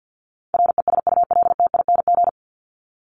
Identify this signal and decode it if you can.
Morse code, heard as a keyed tone, about 50 wpm, 700 Hz. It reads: RESULTING
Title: ING Suffix - Top 100 Words with Base Words 50wpm